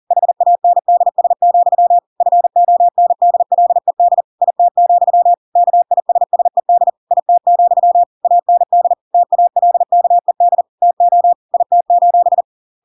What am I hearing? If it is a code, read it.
Morse code, heard as a keyed tone, about 40 wpm, 700 Hz. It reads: HANDS, FONDLED IT, KISSED IT, AND TALKED TO IT:
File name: Pinocchio / Pinocchio ch5 40 40 ik1hge